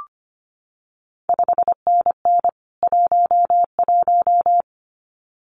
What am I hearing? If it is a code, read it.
Morse code, heard as a keyed tone, about 25 wpm, 700 Hz. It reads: 5NN 11
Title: CQ WW DX Contest Exchange Encore - 25wpm